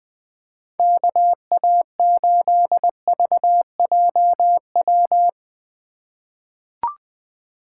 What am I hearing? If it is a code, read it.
Morse code, heard as a keyed tone, about 20 wpm, 700 Hz. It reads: KA8VJW
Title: USA Call Signs - 3x Character Spacing - 20wpm